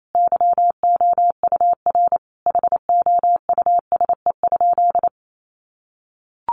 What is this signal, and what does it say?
Morse code, heard as a keyed tone, about 28 wpm, 700 Hz. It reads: YOUR HOUSE?
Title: Sentences from Top 1000 Words 28wpm